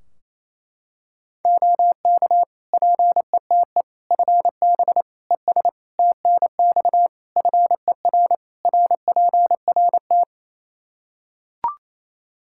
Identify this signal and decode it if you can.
Morse code, heard as a keyed tone, about 28 wpm, 700 Hz. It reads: OK PETE FB ES TNX FER RPRT
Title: Practice QSO - Exchange 2 - Info - 28wpm